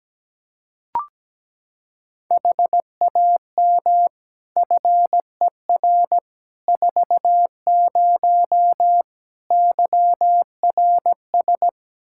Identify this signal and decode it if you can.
Morse code, heard as a keyed tone, about 17 wpm, 700 Hz. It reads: HAM FER 40 YRS
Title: Practice QSO - Exchange 3 - More Info - 17wpm